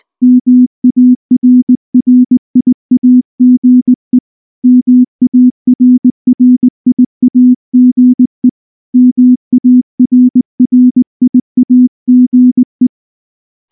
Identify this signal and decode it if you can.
Morse code, heard as a keyed tone, about 19 wpm, 250 Hz. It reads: MARRIAGE MARRIAGE MARRIAGE
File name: morse / morse-training-20WPM-2